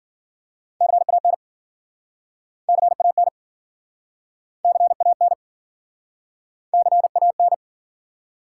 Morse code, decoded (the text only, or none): CAN CAN CAN CAN